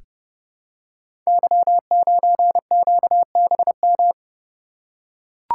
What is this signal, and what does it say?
Morse code, heard as a keyed tone, about 30 wpm, 700 Hz. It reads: Y9QBM
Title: Code Groups - 5 Letter:Numbers - 30wpm